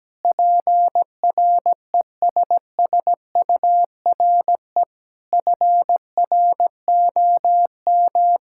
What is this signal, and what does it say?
Morse code, heard as a keyed tone, about 17 wpm, 700 Hz. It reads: PRESSURE FROM
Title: Sentences from Top 1000 Words - Review Part 6 - 17wpm